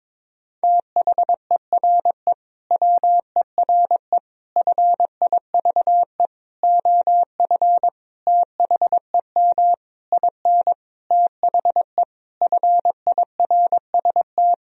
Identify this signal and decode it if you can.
Morse code, heard as a keyed tone, about 22 wpm, 700 Hz. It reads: THERE WERE FIVE OF THEM IN THE FIRST
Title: Sentences from Top 1000 Words - Review Part 2 - 22wpm